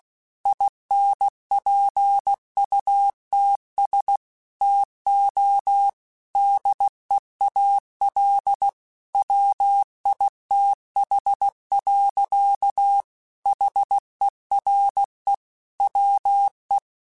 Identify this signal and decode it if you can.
Morse code, heard as a keyed tone, about 16 wpm, 800 Hz. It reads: INPUTS TO DEAL WITH. HERE WE
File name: Technical Subjects / Technical Subjects-mp3s / AND Gate @15